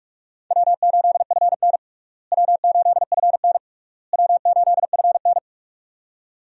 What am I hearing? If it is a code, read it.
Morse code, heard as a keyed tone, about 46 wpm, 700 Hz. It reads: W8FN W8FN W8FN
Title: USA Call Signs - 4 Characters - Repeated 3x - 45wpm